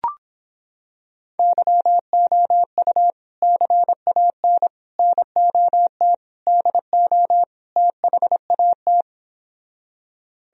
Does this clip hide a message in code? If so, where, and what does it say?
Yes: Morse code from 1.4 to 9.0 s, heard as a keyed tone, about 26 wpm, 700 Hz. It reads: YOU CAN NOT DO THAT